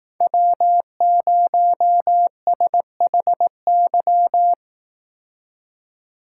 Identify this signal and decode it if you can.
Morse code, heard as a keyed tone, about 18 wpm, 700 Hz. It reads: W0SHY
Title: Speed Racing - USA 1x3 Call Signs 15wpm